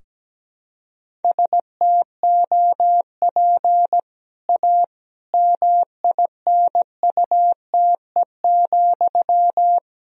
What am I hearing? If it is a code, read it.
Morse code, heard as a keyed tone, about 17 wpm, 700 Hz. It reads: STOP A MINUTE,
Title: Sentences from Top 1000 Words - Review Part 2 - 17wpm